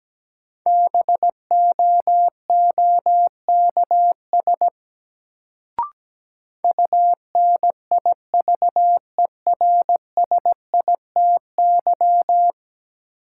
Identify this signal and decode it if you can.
Morse code, heard as a keyed tone, about 17 wpm, 700 Hz. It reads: BOOKS UNIVERSITY
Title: Sentences from Top 1000 Words - Review Part 4 - New Word Drill - 17wpm